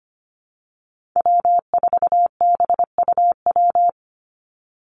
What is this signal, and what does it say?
Morse code, heard as a keyed tone, about 25 wpm, 700 Hz. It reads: W4BUW
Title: Call Signs Repeated 2x - 25wpm